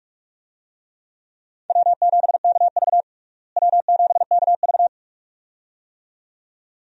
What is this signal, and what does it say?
Morse code, heard as a keyed tone, about 42 wpm, 700 Hz. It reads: W7KV W7KV
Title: USA Call Signs - 4 Characters - Repeated 2x - 45wpm